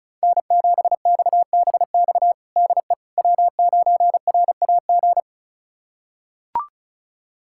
Morse code, heard as a keyed tone, about 36 wpm, 700 Hz. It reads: N7XBX DE W9RAG
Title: USA Callsign DE Callsign - 5 Characters - 35wpm